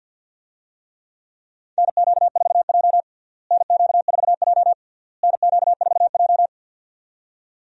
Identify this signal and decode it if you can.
Morse code, heard as a keyed tone, about 52 wpm, 700 Hz. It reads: NQ4J NQ4J NQ4J
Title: USA Call Signs - 4 Characters - Repeated 3x - 50wpm